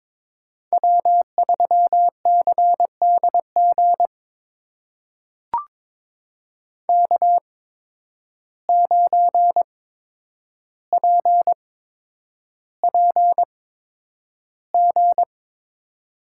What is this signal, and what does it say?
Morse code, heard as a keyed tone, about 22 wpm, spaced as for 9 wpm, 700 Hz. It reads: W3CDG K9PPG
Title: USA Call Signs - 5 Characters - 8x Character Spacing - 22wpm